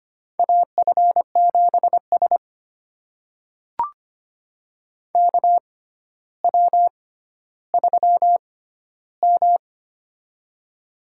Morse code, heard as a keyed tone, about 25 wpm, spaced as for 13 wpm, 700 Hz. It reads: AF7S KW3M
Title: USA Call Signs - 4 Characters - 6x Character Spacing - 25wpm